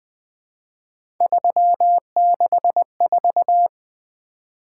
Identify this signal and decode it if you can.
Morse code, heard as a keyed tone, about 20 wpm, 700 Hz. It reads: 364
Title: Three Letter-Number Combinations - 20wpm